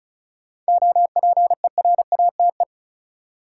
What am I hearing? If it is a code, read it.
Morse code, heard as a keyed tone, about 36 wpm, 700 Hz. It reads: OPERATE
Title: Sentences from Top 1500 Words - New Word Drill - 35wpm